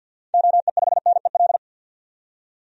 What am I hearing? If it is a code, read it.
Morse code, heard as a keyed tone, about 53 wpm, 700 Hz. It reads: OE5NEL
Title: DXCC-100 Call Signs - 6 Characters - 50wpm